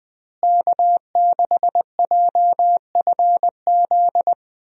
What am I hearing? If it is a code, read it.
Morse code, heard as a keyed tone, about 21 wpm, 700 Hz. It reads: K6JFZ
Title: USA Call Signs - 5 Characters - Repeated 2x - 20wpm